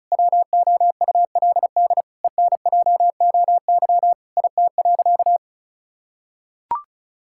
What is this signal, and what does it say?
Morse code, heard as a keyed tone, about 36 wpm, 700 Hz. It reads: WOULD ENJOY IT.
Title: Sentences from Top 1000 Words - Review Part 5 - 35wpm